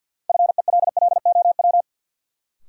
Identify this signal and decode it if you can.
Morse code, heard as a keyed tone, about 52 wpm, 700 Hz. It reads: FELLOW
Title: Sets of 5 Words Encore - New Word Drill - 50wpm